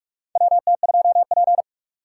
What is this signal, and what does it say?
Morse code, heard as a keyed tone, about 46 wpm, 700 Hz. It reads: WT2P
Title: USA Call Signs - 4 Characters - 6x Character Spacing - 45wpm